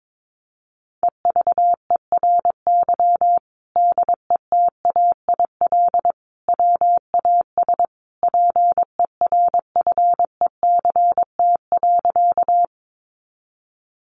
Morse code, heard as a keyed tone, about 22 wpm, 700 Hz. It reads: EVERY DETAIL WAS PERFECT.
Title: Sentences from Top 1500 Words 22wpm